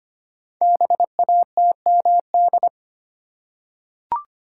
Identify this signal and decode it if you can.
Morse code, heard as a keyed tone, about 25 wpm, 700 Hz. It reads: BATMD
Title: Code Groups - 5 Letters - 25wpm